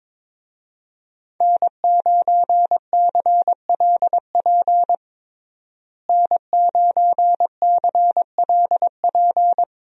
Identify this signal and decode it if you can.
Morse code, heard as a keyed tone, about 21 wpm, 700 Hz. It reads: N9CLP N9CLP
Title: USA Call Signs - 5 Characters - Repeated 2x - 22wpm